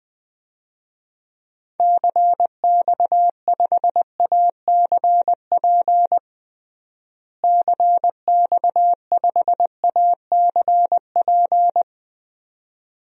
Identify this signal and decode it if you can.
Morse code, heard as a keyed tone, about 20 wpm, 700 Hz. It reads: CX5ACP CX5ACP
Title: DXCC-100 Call Signs - 6 Characters - Repeated 2x - 20wpm